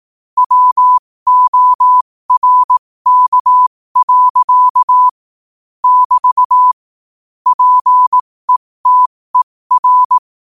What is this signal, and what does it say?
Morse code, heard as a keyed tone, about 18 wpm, spaced as for 15 wpm, 1000 Hz. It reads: WORK. <BT> PETER
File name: 15wpm